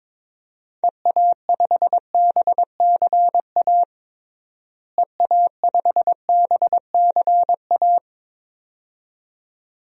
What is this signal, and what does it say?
Morse code, heard as a keyed tone, about 21 wpm, 700 Hz. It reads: EA5BCA EA5BCA
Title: DXCC-100 Call Signs - 6 Characters - Repeated 2x - 22wpm